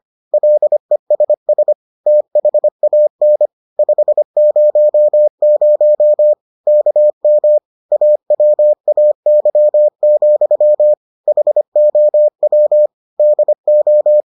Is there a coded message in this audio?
Yes: Morse code from 0.3 to 14.2 s, heard as a keyed tone, about 25 wpm, 600 Hz. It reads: LESS THAN 500 KM AWAY, HOW DO